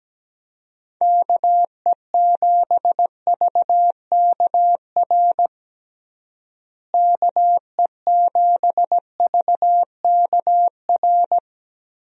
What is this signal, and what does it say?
Morse code, heard as a keyed tone, about 17 wpm, 700 Hz. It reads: KE7VKR KE7VKR
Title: USA Call Signs - 6 Characters - Repeated 3x - 17wpm